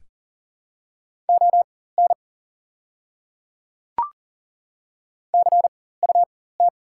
Morse code, heard as a keyed tone, about 40 wpm, spaced as for 19 wpm, 700 Hz. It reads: ON CUT
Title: 4x Character Spacing - Top 500 Words - 40wpm